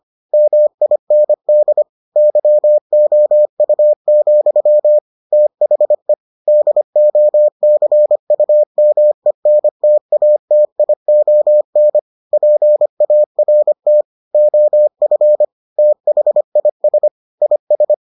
Morse code, heard as a keyed tone, about 25 wpm, 600 Hz. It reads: MIND YOU, THE DOCUMENTATION PART OF THIS IS